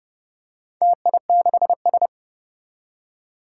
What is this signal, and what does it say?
Morse code, heard as a keyed tone, about 30 wpm, 700 Hz. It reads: TI6S
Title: Four Challenging Letter-Number Combinations 30wpm